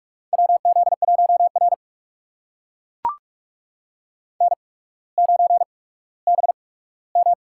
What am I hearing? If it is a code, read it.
Morse code, heard as a keyed tone, about 44 wpm, spaced as for 19 wpm, 690 Hz. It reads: WZ1R N9BM
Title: USA Call Signs - 4 Characters - 8x Character Spacing - 45wpm